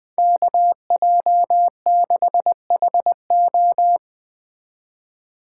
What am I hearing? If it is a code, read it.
Morse code, heard as a keyed tone, about 19 wpm, 700 Hz. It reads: KJ6HO